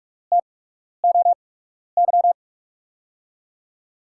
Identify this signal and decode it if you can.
Morse code, heard as a keyed tone, about 47 wpm, spaced as for 12 wpm, 700 Hz. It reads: TOY
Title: USA Call Signs - 8x Character Spacing - 45wpm